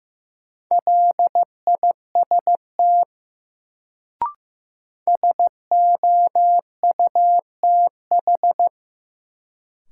Morse code, heard as a keyed tone, about 15 wpm, 700 Hz. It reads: LIST SOUTH
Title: Sentences from Top 400 Words - New Word Drill - 15wpm